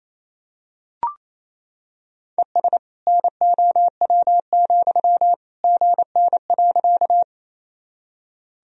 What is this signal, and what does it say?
Morse code, heard as a keyed tone, about 28 wpm, 700 Hz. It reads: ES NOW, GN.
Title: Sentences from Top 300 Words and Abbreviations 28wpm